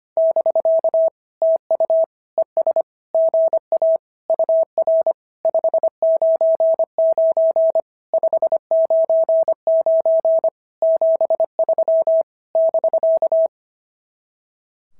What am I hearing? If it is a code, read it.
Morse code, heard as a keyed tone, about 24 wpm, 650 Hz. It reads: <BK> TU ES GA UR 599 599 73 <BK>
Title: SOTA QSO Practice 25wpm